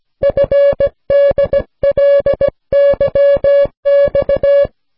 Morse code, heard as a keyed tone, about 17 wpm, 560 Hz. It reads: FDLYX